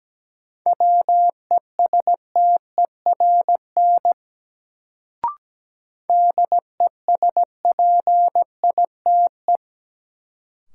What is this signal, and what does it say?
Morse code, heard as a keyed tone, about 17 wpm, 700 Hz. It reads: WESTERN DESPITE
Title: Sentences from Top 1000 Words - Review Part 7 - New Word Drill - 17wpm